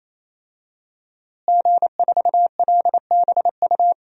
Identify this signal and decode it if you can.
Morse code, heard as a keyed tone, about 28 wpm, 700 Hz. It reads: G4LBU
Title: DXCC-100 Call Signs - 5 Characters - Repeated 3x - 28wpm